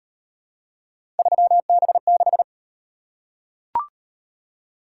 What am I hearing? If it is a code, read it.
Morse code, heard as a keyed tone, about 35 wpm, 700 Hz. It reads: 3B6